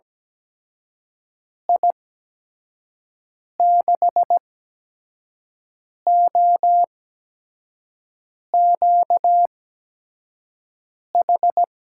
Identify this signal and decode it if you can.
Morse code, heard as a keyed tone, about 17 wpm, spaced as for 5 wpm, 700 Hz. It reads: I6OQH
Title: USA Call Signs - 6 Characters - 8x Character Spacing - 17wpm